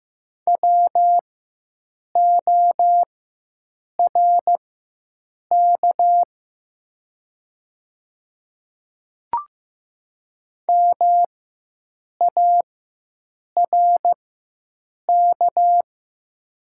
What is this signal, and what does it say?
Morse code, heard as a keyed tone, about 15 wpm, spaced as for 7 wpm, 700 Hz. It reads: WORK MARK
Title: 4x Character Spacing - Top 300 Words - 15wpm